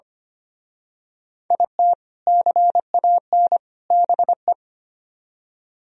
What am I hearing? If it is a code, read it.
Morse code, heard as a keyed tone, about 29 wpm, 700 Hz. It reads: IT CAN BE
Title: Speed-Racing - Sets of 3 Words 25wpm